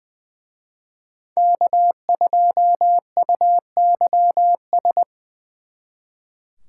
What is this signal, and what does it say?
Morse code, heard as a keyed tone, about 24 wpm, 700 Hz. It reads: K2UYS
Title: Speed Racing - USA 1x3 Call Signs 20wpm